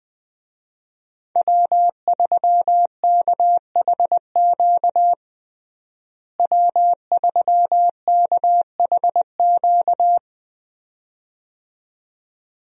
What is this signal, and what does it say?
Morse code, heard as a keyed tone, about 20 wpm, 700 Hz. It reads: W3KHQ W3KHQ
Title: USA Call Signs - 5 Characters - Repeated 2x - 20wpm